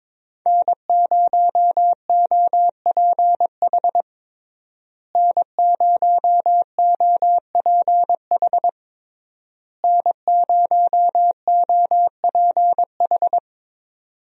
Morse code, heard as a keyed tone, about 22 wpm, 700 Hz. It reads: N0OPH N0OPH N0OPH
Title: USA Call Signs - 5 Characters - Repeated 3x - 22wpm